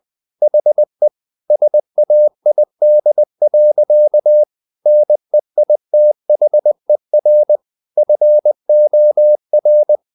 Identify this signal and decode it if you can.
Morse code, heard as a keyed tone, about 20 wpm, 600 Hz. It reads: HE SAID. NEITHER FOR
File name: the old man and the sea part 2 20wpm